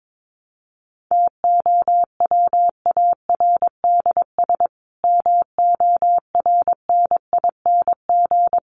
Morse code, heard as a keyed tone, about 22 wpm, 700 Hz. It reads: TOWARDS MORNING
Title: Sentences from Top 1100 Words 22wpm